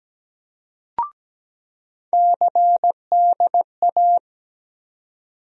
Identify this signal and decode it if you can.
Morse code, heard as a keyed tone, about 17 wpm, 700 Hz. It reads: CDA
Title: USA Call Sign Postfixes - 3 Letters 17wpm